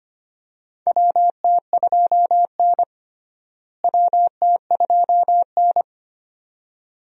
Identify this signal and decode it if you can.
Morse code, heard as a keyed tone, about 25 wpm, 700 Hz. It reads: WT2N WT2N
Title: USA Call Signs - 4 Characters - Repeated 2x - 25wpm